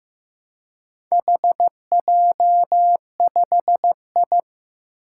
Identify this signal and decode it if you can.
Morse code, heard as a keyed tone, about 15 wpm, 700 Hz. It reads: HJ5I
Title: Four Challenging Letter-Number Combinations 15wpm